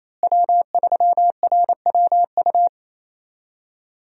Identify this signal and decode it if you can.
Morse code, heard as a keyed tone, about 28 wpm, 700 Hz. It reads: W3RWU
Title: USA Call Signs - 5 Characters - Repeated 3x - 28wpm